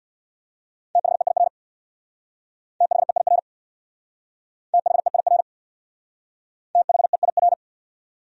Morse code, heard as a keyed tone, about 65 wpm, 700 Hz. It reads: THEIR THEIR THEIR THEIR